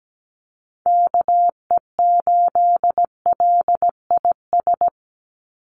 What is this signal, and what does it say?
Morse code, heard as a keyed tone, about 17 wpm, 700 Hz. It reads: KE8LIS
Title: USA Call Signs 17wpm